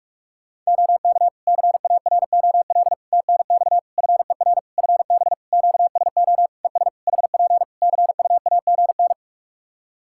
Morse code, heard as a keyed tone, about 46 wpm, 700 Hz. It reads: OK CAROL TNX FER FB QSO ES HP CUAGN
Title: Practice QSO - Exchange 4 - Ending - 45wpm